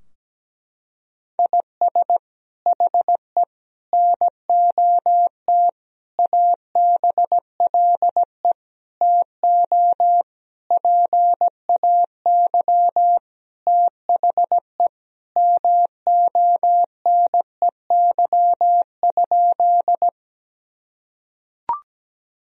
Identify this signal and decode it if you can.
Morse code, heard as a keyed tone, about 17 wpm, 700 Hz. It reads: IS HE NOT ABLE TO PAY THE MONEY?
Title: Sentences from Top 1000 Words - Review Part 2 - 17wpm